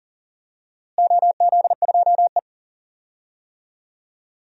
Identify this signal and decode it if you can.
Morse code, heard as a keyed tone, about 40 wpm, 700 Hz. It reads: OZ2E
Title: DXCC-100 Call Signs - 4 Characters - Repeated 3x - 40wpm